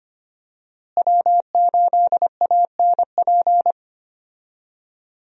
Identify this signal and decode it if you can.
Morse code, heard as a keyed tone, about 25 wpm, 700 Hz. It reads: W8ANP
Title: USA Call Signs - 5 Characters - Repeated 2x - 25wpm